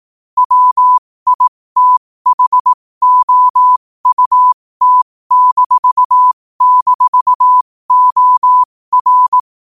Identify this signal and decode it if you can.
Morse code, heard as a keyed tone, about 18 wpm, spaced as for 15 wpm, 1000 Hz. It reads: WITHOUT--OR
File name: qotdfiles / 15wpm